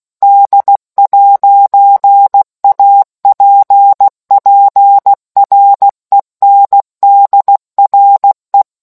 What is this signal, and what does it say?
Morse code, heard as a keyed tone, about 16 wpm, 800 Hz. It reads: D'APPRENDRE